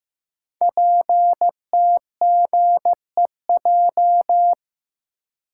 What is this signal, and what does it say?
Morse code, heard as a keyed tone, about 15 wpm, 700 Hz. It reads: PTGEJ
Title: Code Groups - 5 Letters - 15wpm